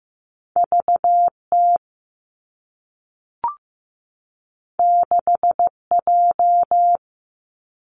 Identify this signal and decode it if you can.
Morse code, heard as a keyed tone, about 15 wpm, 700 Hz. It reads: V T 6 J
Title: Two Challenging Letter-Number Combinations 15wpm